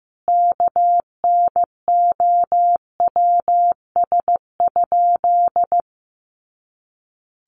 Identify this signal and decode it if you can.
Morse code, heard as a keyed tone, about 15 wpm, 700 Hz. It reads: KNOWS?
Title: A Sound Of Thunder 15wpm